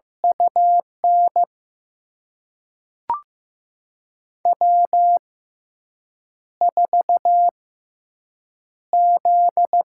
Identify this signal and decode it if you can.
Morse code, heard as a keyed tone, about 15 wpm, spaced as for 4 wpm, 700 Hz. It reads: U NW4Z
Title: USA Call Signs - 6x Character Spacing - 15wpm